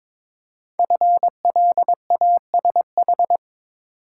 Morse code, heard as a keyed tone, about 22 wpm, 690 Hz. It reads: FLASH
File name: 5 Letter Words - First 3 Letters Given - 22wpm